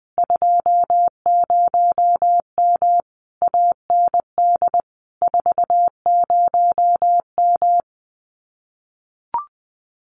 Morse code, heard as a keyed tone, about 19 wpm, 700 Hz. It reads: 20M AND 40M
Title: Common QSO Phrases Encore 20wpm